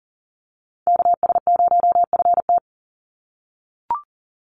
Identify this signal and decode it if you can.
Morse code, heard as a keyed tone, about 40 wpm, 700 Hz. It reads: KS0FT
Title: USA Call Signs 40wpm